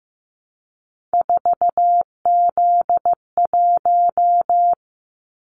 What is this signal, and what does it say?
Morse code, heard as a keyed tone, about 15 wpm, 700 Hz. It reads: 4Z1
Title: Callsign Prefixes for Top 50 DXCC - 15wpm